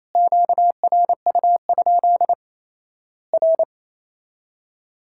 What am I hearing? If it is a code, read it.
Morse code, heard as a keyed tone, about 27 wpm, 700 Hz. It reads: QRU? R
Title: Worked All Europe DX Contest - QTC Exchange - 28wpm